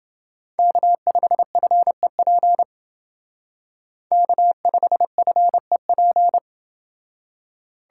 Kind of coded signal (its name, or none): Morse code